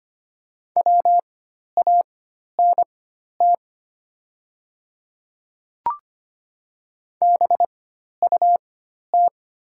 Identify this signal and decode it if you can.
Morse code, heard as a keyed tone, about 25 wpm, spaced as for 12 wpm, 700 Hz. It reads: WANT BUT
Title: 4x Character Spacing - Top 500 Words - 25wpm